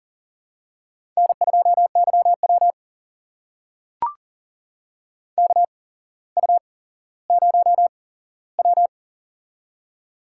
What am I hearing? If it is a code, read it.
Morse code, heard as a keyed tone, about 41 wpm, spaced as for 17 wpm, 700 Hz. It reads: N2YW KU0W
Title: USA Call Signs - 4 Characters - 8x Character Spacing - 40wpm